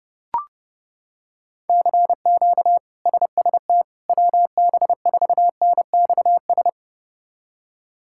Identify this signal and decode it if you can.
Morse code, heard as a keyed tone, about 28 wpm, 700 Hz. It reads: CQ SST WB4NXS
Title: K1USN SST QSO Practice - 30wpm